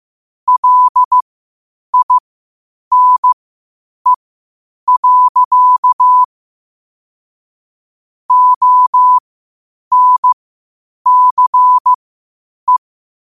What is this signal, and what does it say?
Morse code, heard as a keyed tone, about 15 wpm, spaced as for 8 wpm, 1000 Hz. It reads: LINE. ONCE
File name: qotdfiles / 8wpm